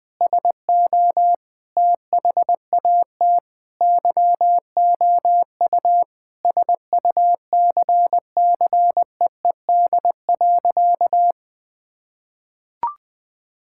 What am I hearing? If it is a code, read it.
Morse code, heard as a keyed tone, about 20 wpm, 700 Hz. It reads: SO THAT YOU SUCCEED.
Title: Sentences with 2-Word Phrases - Spoken Beginning - 20wpm